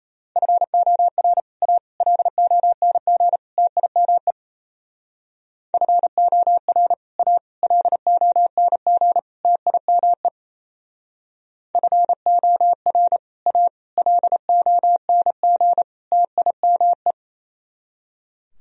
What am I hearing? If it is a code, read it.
Morse code, heard as a keyed tone, about 34 wpm, 700 Hz. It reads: FOR A LONG TIME FOR A LONG TIME FOR A LONG TIME
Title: Speed-Racing - Sets of 4 Words 28wpm